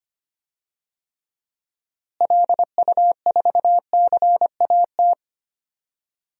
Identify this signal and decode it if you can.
Morse code, heard as a keyed tone, about 25 wpm, 700 Hz. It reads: LU4CAT
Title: DXCC-100 Call Signs - 6 Characters - Repeated 2x - 25wpm